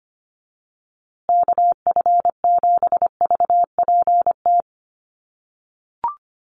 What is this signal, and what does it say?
Morse code, heard as a keyed tone, about 25 wpm, 700 Hz. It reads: KF7VPT
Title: USA Call Signs Repeated 3x 25wpm